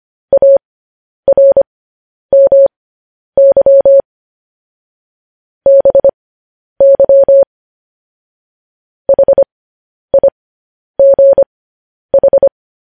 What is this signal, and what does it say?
Morse code, heard as a keyed tone, about 25 wpm, spaced as for 10 wpm, 550 Hz. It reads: ARMY BY HIGH